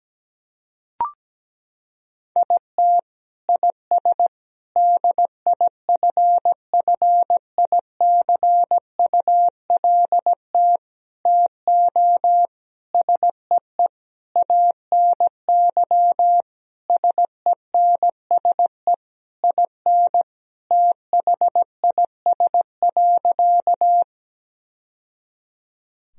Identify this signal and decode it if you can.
Morse code, heard as a keyed tone, about 17 wpm, 700 Hz. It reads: IT IS DIFFICULT TO SEE ANY SENSE IN THIS.
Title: Sentences from Top 1000 Words - Review Part 4 - 17wpm